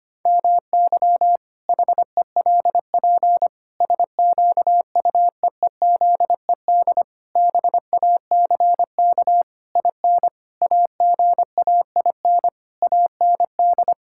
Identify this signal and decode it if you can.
Morse code, heard as a keyed tone, about 25 wpm, 700 Hz. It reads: MY HELP SQUEEZED BACK IN AGAIN AND
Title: Pangram Challenge v2 - 25wpm